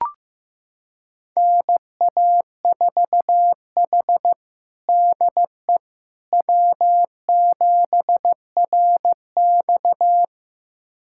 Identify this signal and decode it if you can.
Morse code, heard as a keyed tone, about 16 wpm, 700 Hz. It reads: NA4H DE W7RX
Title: USA Callsign DE Callsign - 4 Characters - 15wpm